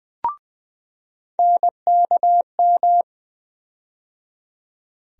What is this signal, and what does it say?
Morse code, heard as a keyed tone, about 20 wpm, 700 Hz. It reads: NKM
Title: USA Call Sign Postfixes - 3 Letters 20wpm